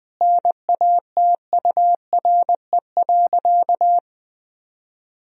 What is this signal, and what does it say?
Morse code, heard as a keyed tone, about 20 wpm, 700 Hz. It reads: NATURE.
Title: Sentences from Top 1000 Words - Review Part 4 - 20wpm